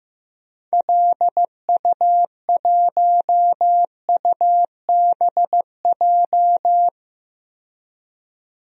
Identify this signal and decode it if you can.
Morse code, heard as a keyed tone, about 15 wpm, 700 Hz. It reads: LU1UBJ
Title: DXCC-100 Call Signs - 6 Characters - 15wpm